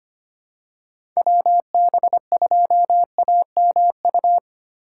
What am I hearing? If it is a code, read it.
Morse code, heard as a keyed tone, about 25 wpm, 700 Hz. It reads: WB2AMU
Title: USA Call Signs - 6 Characters - 4x Character Spacing - 25wpm